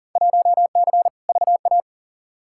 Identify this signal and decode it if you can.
Morse code, heard as a keyed tone, about 40 wpm, 700 Hz. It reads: 1C VA
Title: Field Day Exchanges Encore - 40wpm